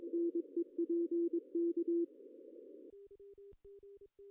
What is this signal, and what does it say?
Morse code, heard as a keyed tone, about 21 wpm, 340 Hz. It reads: NEPK
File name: CW / cqww-cw-k5tr